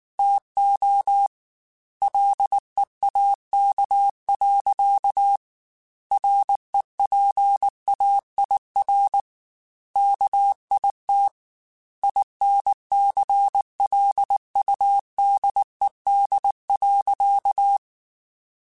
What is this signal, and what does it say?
Morse code, heard as a keyed tone, about 19 wpm, 800 Hz. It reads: TO LEAK. REPAIR KIT INCLUDED.
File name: Chuckles / Chuckles-mp3s / Guaranteed @18